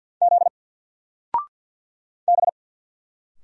Morse code, heard as a keyed tone, about 46 wpm, 700 Hz. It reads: Z B
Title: Single Letter-Number 50wpm